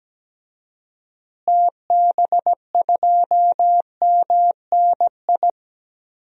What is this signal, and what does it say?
Morse code, heard as a keyed tone, about 17 wpm, 700 Hz. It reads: TB2MNI
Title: DXCC-100 Call Signs - 6 Characters - Repeated 2x - 17wpm